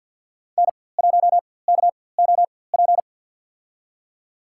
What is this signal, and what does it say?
Morse code, heard as a keyed tone, about 51 wpm, spaced as for 23 wpm, 700 Hz. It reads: N1KOP
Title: USA Call Signs - 5 Characters - 4x Character Spacing - 50wpm